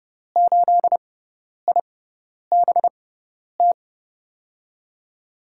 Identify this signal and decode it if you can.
Morse code, heard as a keyed tone, about 32 wpm, spaced as for 10 wpm, 700 Hz. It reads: 8IBT
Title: USA Call Signs - 6 Characters - 6x Character Spacing - 30wpm